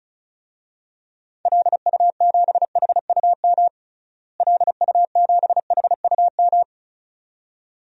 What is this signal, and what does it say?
Morse code, heard as a keyed tone, about 36 wpm, 700 Hz. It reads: LU7HUM LU7HUM
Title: DXCC-100 Call Signs - 6 Characters - Repeated 2x - 35wpm